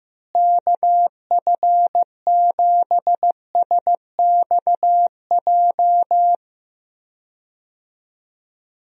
Morse code, heard as a keyed tone, about 15 wpm, 700 Hz. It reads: KF7SXJ
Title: USA Call Signs - 6 Characters - Repeated 2x - 15wpm